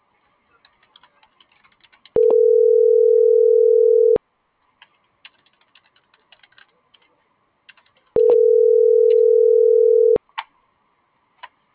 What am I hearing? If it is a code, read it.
Ringback tone (North America), 440 + 480 Hz.
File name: CENTRAL GENERAL / out karla.sky - .14018